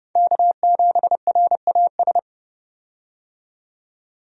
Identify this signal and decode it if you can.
Morse code, heard as a keyed tone, about 30 wpm, 700 Hz. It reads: K7RAS
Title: USA Call Signs - 5 Characters - 30wpm